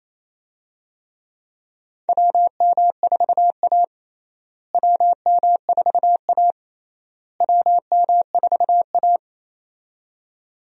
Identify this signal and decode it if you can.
Morse code, heard as a keyed tone, about 28 wpm, 700 Hz. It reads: WM4A WM4A WM4A
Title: USA Call Signs - 4 Characters - Repeated 3x - 28wpm